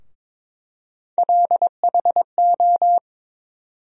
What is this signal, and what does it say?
Morse code, heard as a keyed tone, about 22 wpm, 700 Hz. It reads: LHO
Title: USA Call Sign Postfixes - 3 Letters 22wpm